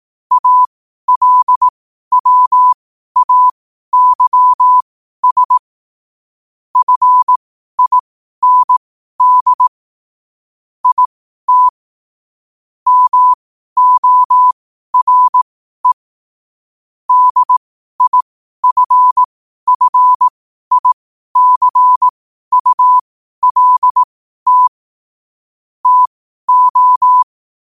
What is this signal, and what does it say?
Morse code, heard as a keyed tone, about 18 wpm, spaced as for 12 wpm, 1000 Hz. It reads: ALWAYS FIND IT MORE DIFFICULT TO
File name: qotdfiles / 12wpm